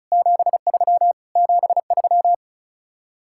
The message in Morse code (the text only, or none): 73 73